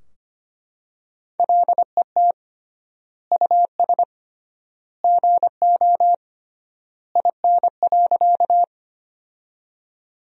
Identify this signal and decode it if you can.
Morse code, heard as a keyed tone, about 25 wpm, 700 Hz. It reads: LET US GO IN.
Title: 3x Word-Spacing - Sentences from Top 300 Words - 25wpm